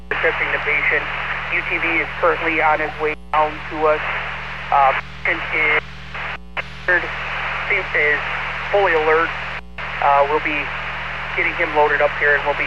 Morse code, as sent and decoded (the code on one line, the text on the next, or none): none